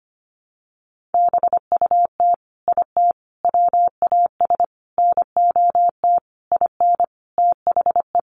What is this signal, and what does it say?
Morse code, heard as a keyed tone, about 25 wpm, 700 Hz. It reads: BUT IT WAS NOT IN THE
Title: Sentences from Top 1500 Words 25wpm